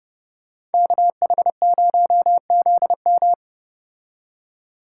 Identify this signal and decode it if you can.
Morse code, heard as a keyed tone, about 29 wpm, 700 Hz. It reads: KH0ZM